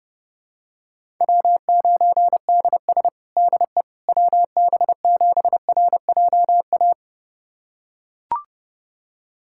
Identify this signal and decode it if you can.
Morse code, heard as a keyed tone, about 30 wpm, 700 Hz. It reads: W9DS DE WB7RJA
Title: USA Callsign DE Callsign 30wpm